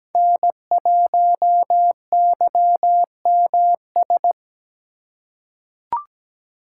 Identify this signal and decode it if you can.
Morse code, heard as a keyed tone, about 17 wpm, 700 Hz. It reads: N1YMS
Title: USA Call Signs - 2x Character Spacing - 17wpm